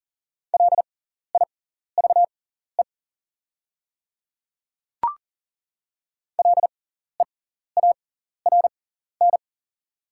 Morse code, heard as a keyed tone, about 40 wpm, spaced as for 14 wpm, 690 Hz. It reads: LIVE LEARN
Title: 6x Character Spacing - Top 300 Words - 40wpm